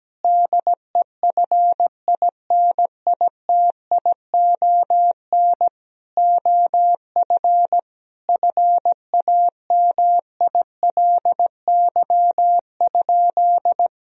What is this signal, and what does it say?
Morse code, heard as a keyed tone, about 17 wpm, 700 Hz. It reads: DEFINITION OF FAMILY?
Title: Sentences from Top 1000 Words - Review Part 4 - 17wpm